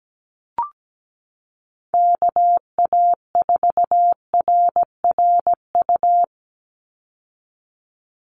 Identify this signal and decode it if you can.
Morse code, heard as a keyed tone, about 17 wpm, 700 Hz. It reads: KA4RRU
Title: Call Signs Repeated 3x - 17wpm